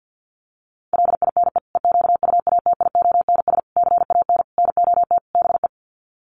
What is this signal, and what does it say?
Morse code, heard as a keyed tone, about 50 wpm, 700 Hz. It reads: LINE EQUATIONS CAN NOT BE
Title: Sets of 5 Words 50wpm